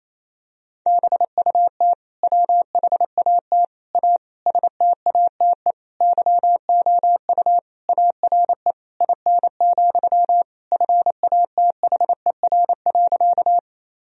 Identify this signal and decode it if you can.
Morse code, heard as a keyed tone, about 27 wpm, 700 Hz. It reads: BUT WHAT A STATE YOU ARE IN, FATHER.
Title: Sentences from Top 1000 Words - Review Part 2 - 28wpm